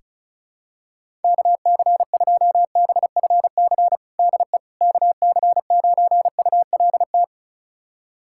Morse code, heard as a keyed tone, about 34 wpm, 700 Hz. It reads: KC2BFC DE KC9ULT
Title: USA Callsign DE Callsign - 6 Characters - 35wpm